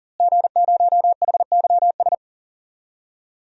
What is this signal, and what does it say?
Morse code, heard as a keyed tone, about 40 wpm, 700 Hz. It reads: G0HYS
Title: DXCC-100 Call Signs - 5 Characters - Repeated 3x - 40wpm